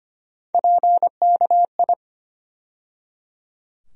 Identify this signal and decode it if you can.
Morse code, heard as a keyed tone, about 25 wpm, 700 Hz. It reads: PKI
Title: USA Call Sign Postfixes - 3 Letters 25wpm